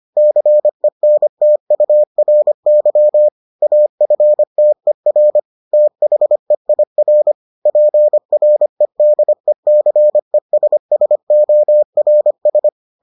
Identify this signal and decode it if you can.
Morse code, heard as a keyed tone, about 25 wpm, 600 Hz. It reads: CENTURY AFTER THEIR PREDECESSORS